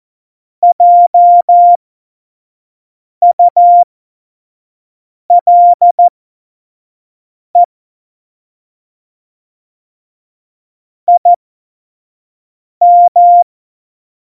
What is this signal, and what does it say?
Morse code, heard as a keyed tone, about 14 wpm, spaced as for 5 wpm, 700 Hz. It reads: JULE IM
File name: Morsekurs / Witze / Witz 11. Juni 2019, 25 F70